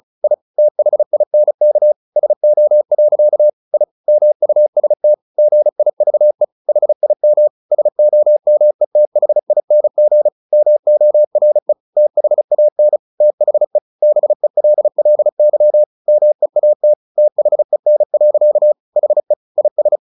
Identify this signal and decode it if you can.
Morse code, heard as a keyed tone, about 34 wpm, 600 Hz. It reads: I THINK SO. I MUST GIVE HIM SOMETHING MORE THAN THE BELLY MEAT THEN. HE IS
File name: the old man and the sea / the old man and the sea 1 35wpm